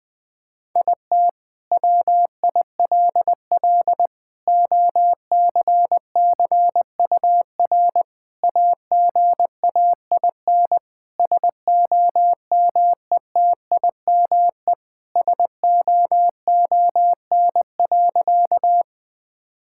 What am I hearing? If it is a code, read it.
Morse code, heard as a keyed tone, about 21 wpm, 700 Hz. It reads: IT WILL OCCUR AGAIN SOMETIME SOON.
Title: Sentences from Top 1000 Words - Review Part 6 - 20wpm